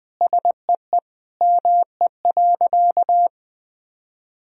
Morse code, heard as a keyed tone, about 20 wpm, 700 Hz. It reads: SEE ME.
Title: Sentences from Top 1000 Words - Review Part 1 - 20wpm